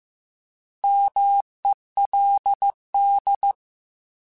Morse code, heard as a keyed tone, about 15 wpm, 800 Hz. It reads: MELD
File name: Alphabetical / Alphabetical mp3s / Ms @13